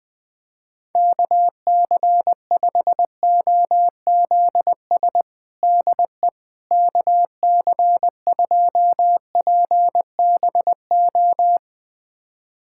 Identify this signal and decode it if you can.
Morse code, heard as a keyed tone, about 20 wpm, 700 Hz. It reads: KC5OZS DE KC2PBO
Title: USA Callsign DE Callsign 20wpm